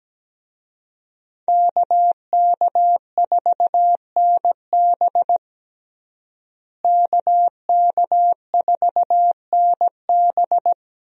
Morse code, heard as a keyed tone, about 17 wpm, 700 Hz. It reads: KK4NB KK4NB
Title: USA Call Signs - 5 Characters - Repeated 3x - 17wpm